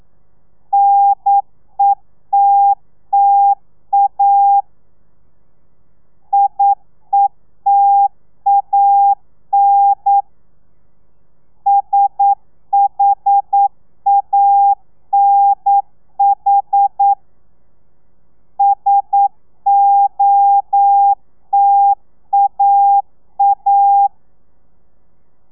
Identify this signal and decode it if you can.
Morse code, heard as a keyed tone, about 9 wpm, 800 Hz. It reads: NETTA IETAN SHANH SOTAA
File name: telegrafia / i2tfj / Lez2 40